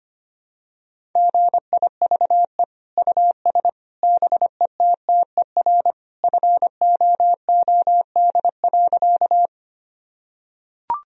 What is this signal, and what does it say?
Morse code, heard as a keyed tone, about 25 wpm, 700 Hz. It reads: GIVE US BETTER FOOD.
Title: Sentences from Top 1000 Words - Review Part 1 - 25wpm